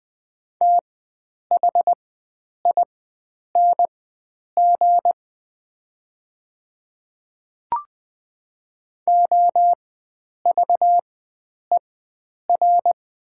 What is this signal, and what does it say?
Morse code, heard as a keyed tone, about 20 wpm, spaced as for 9 wpm, 690 Hz. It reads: THING OVER
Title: 4x Character Spacing - Top 100 Words - 20wpm